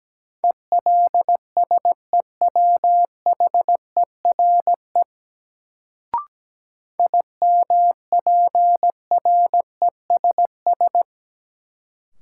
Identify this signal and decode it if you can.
Morse code, heard as a keyed tone, about 17 wpm, 700 Hz. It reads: ELSEWHERE IMPRESS
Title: Sentences from Top 1600 Words - New Word Drill - 17wpm